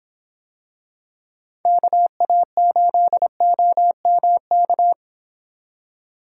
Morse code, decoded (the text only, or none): KA8OMK